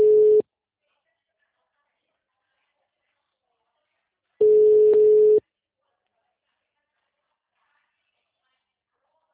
Ringback tone (Europe, 425 Hz).
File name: out- -2012- - - .34354